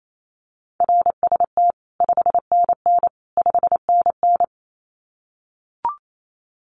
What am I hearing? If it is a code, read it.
Morse code, heard as a keyed tone, about 28 wpm, 700 Hz. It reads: RST 5NN 5NN
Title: Common QSO Phrases Encore 28wpm